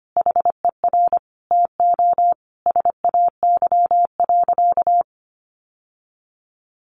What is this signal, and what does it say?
Morse code, heard as a keyed tone, about 25 wpm, 700 Hz. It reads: HER TO SAY.
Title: Sentences from Top 1400 Words 25wpm